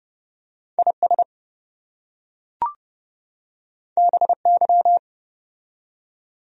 Morse code, heard as a keyed tone, about 28 wpm, 700 Hz. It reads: IS BY